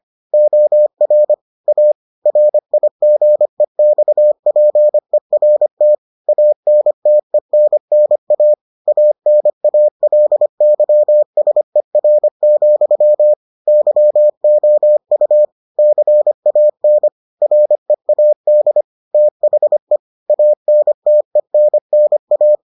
Morse code, heard as a keyed tone, about 25 wpm, 600 Hz. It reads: OR A RIGEXPERT ANTENNA ANALYSER, YOU CAN READ THE ANTENNA